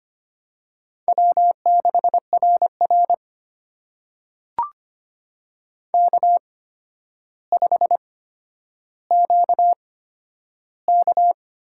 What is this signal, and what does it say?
Morse code, heard as a keyed tone, about 25 wpm, spaced as for 11 wpm, 700 Hz. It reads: W6RR K5QK
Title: USA Call Signs - 4 Characters - 8x Character Spacing - 25wpm